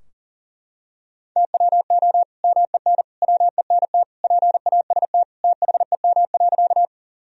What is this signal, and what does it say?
Morse code, heard as a keyed tone, about 40 wpm, 700 Hz. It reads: TWO MEN WENT PAST THEM.
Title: Sentences from Top 1000 Words - Review Part 3 - 40wpm